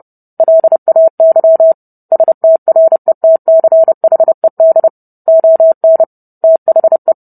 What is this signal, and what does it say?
Morse code, heard as a keyed tone, about 30 wpm, 650 Hz. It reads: LAY STRETCHED ON THE